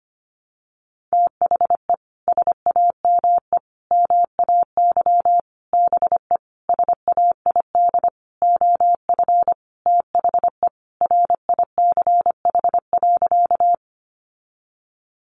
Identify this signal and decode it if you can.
Morse code, heard as a keyed tone, about 25 wpm, 700 Hz. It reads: THE SAME MAY BE SAID OF THE RICH.
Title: Sentences from Top 1100 Words 25wpm